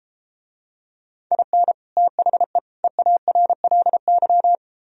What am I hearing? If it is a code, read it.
Morse code, heard as a keyed tone, about 32 wpm, 700 Hz. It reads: IN THE EARLY